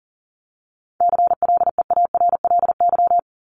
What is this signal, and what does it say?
Morse code, heard as a keyed tone, about 40 wpm, 700 Hz. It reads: CLEARLY
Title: Top 1001-1250 Words 40wpm